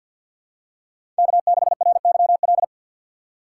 Morse code, heard as a keyed tone, about 47 wpm, 700 Hz. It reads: K6RYL
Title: USA Call Signs - 5 Characters - 3x Character Spacing - 50wpm